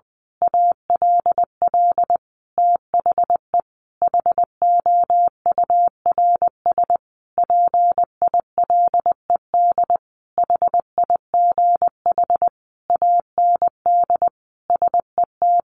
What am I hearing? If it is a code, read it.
Morse code, heard as a keyed tone, about 20 wpm, 700 Hz. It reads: ALL THE HOURS PILED HIGH AND SET
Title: A Sound Of Thunder 20wpm